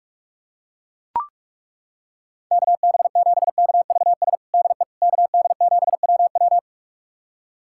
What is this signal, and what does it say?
Morse code, heard as a keyed tone, about 41 wpm, 700 Hz. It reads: KB7KVS DE KD7WW
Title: USA Callsign DE Callsign 45wpm